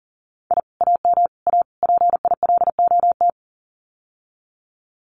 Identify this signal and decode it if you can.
Morse code, heard as a keyed tone, about 40 wpm, 700 Hz. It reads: I AM A PILOT
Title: Common QSO Phrases Encore 40wpm